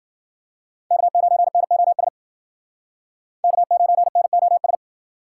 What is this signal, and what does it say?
Morse code, heard as a keyed tone, about 56 wpm, 700 Hz. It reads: K9NOS K9NOS